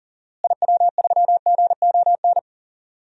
Morse code, heard as a keyed tone, about 38 wpm, 700 Hz. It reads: IW3GON
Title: DXCC-100 Call Signs - 6 Characters - Repeated 3x - 40wpm